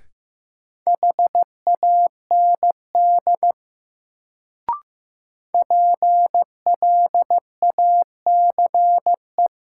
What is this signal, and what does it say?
Morse code, heard as a keyed tone, about 15 wpm, 700 Hz. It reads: HAND PLACE
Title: Sentences from Top 200 Words - New Word Drill - 15wpm